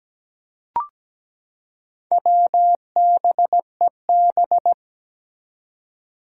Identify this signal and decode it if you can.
Morse code, heard as a keyed tone, about 17 wpm, 700 Hz. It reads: WBEB
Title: Four Challenging Letter-Number Combinations 17wpm